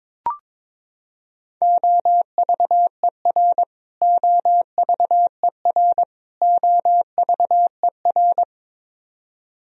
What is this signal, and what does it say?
Morse code, heard as a keyed tone, about 22 wpm, 700 Hz. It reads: OVER OVER OVER
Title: Top 100 Words Repeated 22wpm